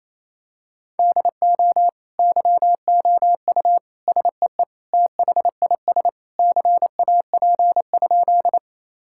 Morse code, heard as a keyed tone, about 27 wpm, 700 Hz. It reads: DO YOU SEE THIS CAP?
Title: Sentences from Top 1600 Words 28wpm